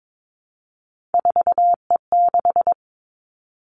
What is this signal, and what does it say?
Morse code, heard as a keyed tone, about 22 wpm, 700 Hz. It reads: 4E6
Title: Three Challenging Letter-Number Combinations 22wpm